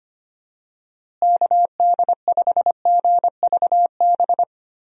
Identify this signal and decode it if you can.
Morse code, heard as a keyed tone, about 25 wpm, 700 Hz. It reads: KD5GVB
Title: USA Call Signs - 6 Characters - Repeated 2x - 25wpm